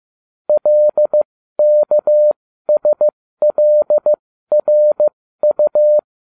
Morse code, heard as a keyed tone, about 15 wpm, spaced as for 13 wpm, 600 Hz. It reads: LKSLRU